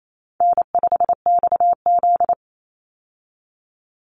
Morse code, heard as a keyed tone, about 28 wpm, 700 Hz. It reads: N5XZ
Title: Easy Call Signs Repeated 2x - 28wpm